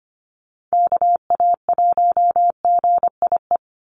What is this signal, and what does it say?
Morse code, heard as a keyed tone, about 25 wpm, 700 Hz. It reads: KA1GIE
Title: USA Call Signs 25wpm